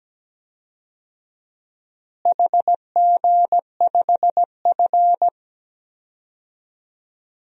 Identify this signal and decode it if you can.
Morse code, heard as a keyed tone, about 17 wpm, 700 Hz. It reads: HG5F
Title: DXCC-100 Call Signs - 4 Characters - Repeated 3x - 17wpm